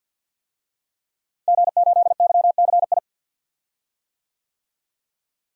Morse code, heard as a keyed tone, about 47 wpm, 700 Hz. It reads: G8YCI
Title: DXCC-100 Call Signs - 5 Characters - Repeated 2x - 50wpm